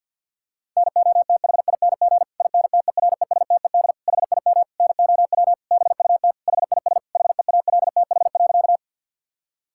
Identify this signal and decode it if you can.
Morse code, heard as a keyed tone, about 50 wpm, 700 Hz. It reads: NOTHING INTERESTED HIM NOW BUT HIS HEALTH.
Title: Sentences from Top 1000 Words - Review Part 6 - 50wpm